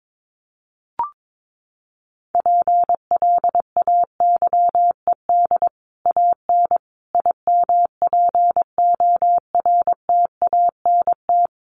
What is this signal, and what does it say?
Morse code, heard as a keyed tone, about 22 wpm, 700 Hz. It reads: PLAYED AN IMPORTANT
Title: Sets of 5 Words Encore2 22wpm